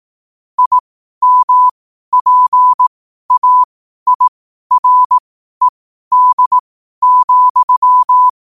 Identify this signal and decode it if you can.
Morse code, heard as a keyed tone, about 18 wpm, spaced as for 12 wpm, 1000 Hz. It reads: IMPAIRED,
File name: qotdfiles / 12wpm